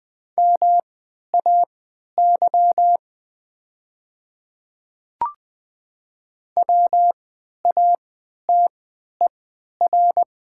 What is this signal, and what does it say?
Morse code, heard as a keyed tone, about 20 wpm, spaced as for 11 wpm, 700 Hz. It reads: MAY WATER
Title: 3x Character Spacing - Top 100 Words - 20wpm